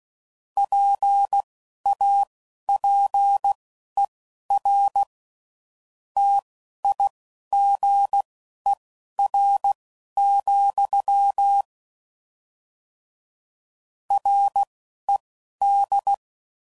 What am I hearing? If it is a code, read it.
Morse code, heard as a keyed tone, about 16 wpm, spaced as for 12 wpm, 800 Hz. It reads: PAPER TIGER, RED